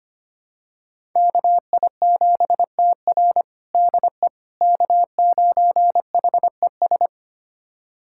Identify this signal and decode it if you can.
Morse code, heard as a keyed tone, about 25 wpm, 700 Hz. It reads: KI7TR DE K9HES
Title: USA Callsign DE Callsign - 5 Characters - 25wpm